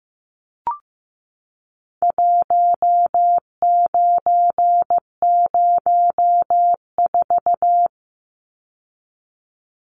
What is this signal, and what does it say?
Morse code, heard as a keyed tone, about 15 wpm, 700 Hz. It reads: 1904
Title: Numbers - 4-Digit 15wpm